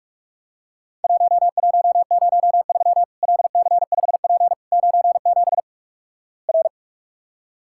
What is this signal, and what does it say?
Morse code, heard as a keyed tone, about 44 wpm, 700 Hz. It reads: 1103 LC5P 97 R
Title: Worked All Europe DX Contest - QTC Exchange - 45wpm